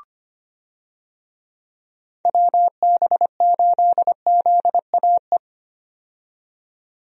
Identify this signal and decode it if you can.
Morse code, heard as a keyed tone, about 25 wpm, 700 Hz. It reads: WB8ZAE
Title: USA Call Signs - 6 Characters - 25wpm